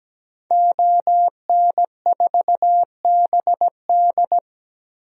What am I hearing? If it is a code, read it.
Morse code, heard as a keyed tone, about 17 wpm, 700 Hz. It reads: ON4BD
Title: DXCC-100 Call Signs - 5 Characters - Repeated 2x - 17wpm